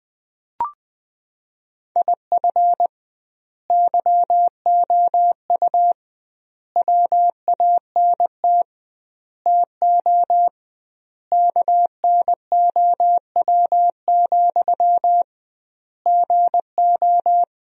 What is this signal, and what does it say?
Morse code, heard as a keyed tone, about 20 wpm, 700 Hz. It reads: IF YOU WANT TO KNOW, GO
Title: 2x Word-Spacing - Sentences from Top 200 Words - 20wpm